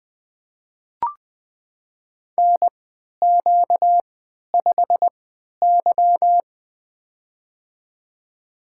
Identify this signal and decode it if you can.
Morse code, heard as a keyed tone, about 20 wpm, spaced as for 11 wpm, 700 Hz. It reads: NQ5Y
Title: USA Call Signs - 4 Characters - 3x Character Spacing - 20wpm